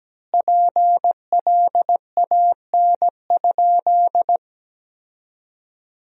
Morse code, heard as a keyed tone, about 17 wpm, 700 Hz. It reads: PLAN?
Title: 2x Word-Spacing - Sentences from Top 500 Words - 17wpm